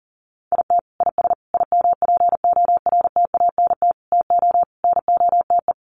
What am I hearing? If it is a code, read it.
Morse code, heard as a keyed tone, about 40 wpm, 700 Hz. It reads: IT IS IMPORTANT TO NOTE
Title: Sets of 5 Words 40wpm